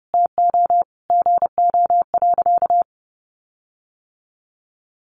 Morse code, heard as a keyed tone, about 30 wpm, 700 Hz. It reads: TO GO.
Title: Sentences from Top 1400 Words 30wpm